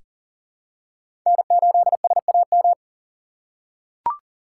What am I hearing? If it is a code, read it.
Morse code, heard as a keyed tone, about 40 wpm, 700 Hz. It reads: N8SAM
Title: USA Call Signs - 5 Characters - 4x Character Spacing - 40wpm